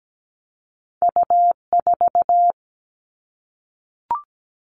Morse code, heard as a keyed tone, about 17 wpm, 700 Hz. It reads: U4
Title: Two Challenging Letter-Number Combinations 17wpm